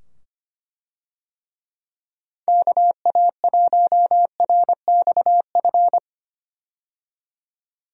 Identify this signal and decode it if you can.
Morse code, heard as a keyed tone, about 25 wpm, 700 Hz. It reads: KA1RXF
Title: USA Call Signs - 6 Characters - 25wpm